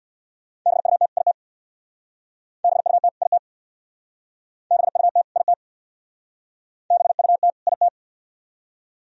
Morse code, heard as a keyed tone, about 65 wpm, 700 Hz. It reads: BUT IT BUT IT BUT IT BUT IT